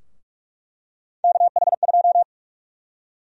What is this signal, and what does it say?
Morse code, heard as a keyed tone, about 46 wpm, 700 Hz. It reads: KH2
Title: USA Call Sign Prefixes - 45wpm